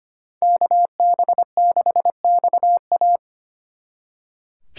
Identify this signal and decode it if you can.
Morse code, heard as a keyed tone, about 27 wpm, 700 Hz. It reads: KB6XA
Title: Speed Racing - USA 2x2 Call Signs 25wpm